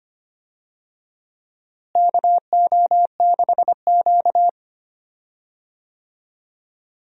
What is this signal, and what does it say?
Morse code, heard as a keyed tone, about 25 wpm, 700 Hz. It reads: KO6Q
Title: USA Call Signs - 4 Characters - Repeated 2x - 25wpm